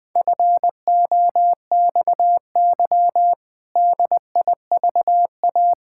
Morse code, heard as a keyed tone, about 20 wpm, 700 Hz. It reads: FOXY DIVA
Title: Pangram Challenge v2 - 20wpm